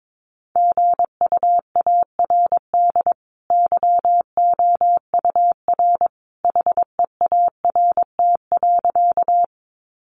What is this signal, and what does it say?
Morse code, heard as a keyed tone, about 22 wpm, 700 Hz. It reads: GUARD YOUR HEART.
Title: Sentences from Top 1200 Words 22wpm